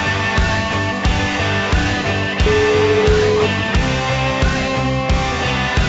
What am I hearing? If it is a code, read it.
Ringback tone (Europe, 425 Hz), with noise.